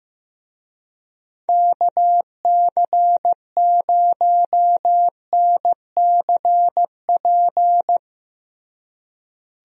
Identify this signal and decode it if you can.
Morse code, heard as a keyed tone, about 15 wpm, 700 Hz. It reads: KC0NCP
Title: USA Call Signs - 6 Characters - Repeated 3x - 15wpm